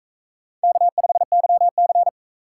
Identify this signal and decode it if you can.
Morse code, heard as a keyed tone, about 44 wpm, 700 Hz. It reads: K5YC